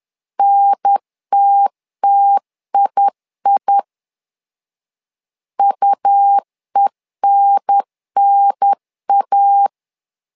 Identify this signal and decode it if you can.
Morse code, heard as a keyed tone, about 10 wpm, 800 Hz. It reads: NTTII UENNA
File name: LEZIONI / 2 UHDN / 2 UHDN 10wpm 1904